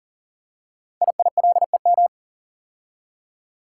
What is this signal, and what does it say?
Morse code, heard as a keyed tone, about 38 wpm, 700 Hz. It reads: IILEM
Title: Code Groups - 5 Letter:Numbers - 40wpm